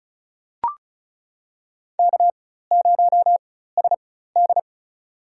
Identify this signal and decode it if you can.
Morse code, heard as a keyed tone, about 36 wpm, spaced as for 16 wpm, 700 Hz. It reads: K0SD
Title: USA Call Signs - 4 Characters - 4x Character Spacing - 35wpm